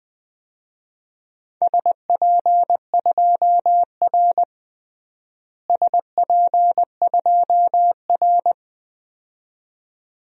Morse code, heard as a keyed tone, about 20 wpm, 700 Hz. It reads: SP2R SP2R
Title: DXCC-100 Call Signs - 4 Characters - Repeated 2x - 20wpm